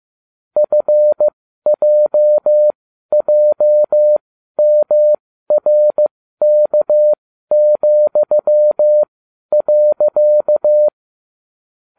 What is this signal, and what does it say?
Morse code, heard as a keyed tone, about 15 wpm, spaced as for 12 wpm, 600 Hz. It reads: FJJMRK,.